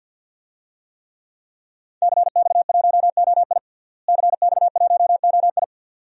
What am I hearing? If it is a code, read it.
Morse code, heard as a keyed tone, about 47 wpm, 700 Hz. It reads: CX1OI CX1OI
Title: DXCC-100 Call Signs - 5 Characters - Repeated 2x - 50wpm